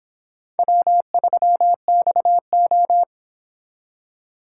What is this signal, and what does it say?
Morse code, heard as a keyed tone, about 30 wpm, 700 Hz. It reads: W3XO